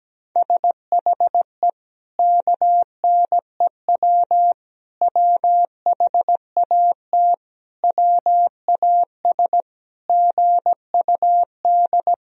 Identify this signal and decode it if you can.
Morse code, heard as a keyed tone, about 17 wpm, 700 Hz. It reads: SHE KNEW WHAT WAS GUD
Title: Sentences from Top 400 Words and Abbreviations 17wpm